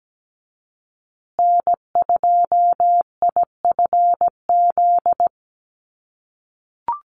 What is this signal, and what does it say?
Morse code, heard as a keyed tone, about 17 wpm, 700 Hz. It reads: N2IFZ
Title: USA Call Signs Repeated 3x 17wpm